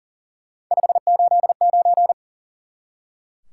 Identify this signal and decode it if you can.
Morse code, heard as a keyed tone, about 40 wpm, 700 Hz. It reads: 589
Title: Top 180 QSO Elements 40wpm